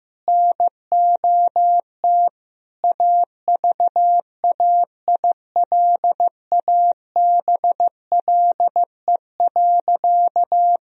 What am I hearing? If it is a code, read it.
Morse code, heard as a keyed tone, about 15 wpm, 700 Hz. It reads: NOT AVAILABLE.
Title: Sentences from Top 1000 Words - Review Part 3 - 15wpm